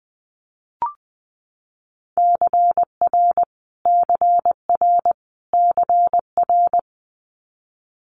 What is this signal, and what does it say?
Morse code, heard as a keyed tone, about 20 wpm, 700 Hz. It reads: CR CR CR
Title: Multi-letter Phonemes Repeated 3x 20wpm